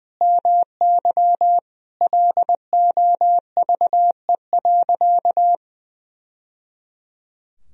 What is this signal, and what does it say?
Morse code, heard as a keyed tone, about 20 wpm, 700 Hz. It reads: MY LOVE.
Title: Sentences from Top 1000 Words - Review Part 3 - 20wpm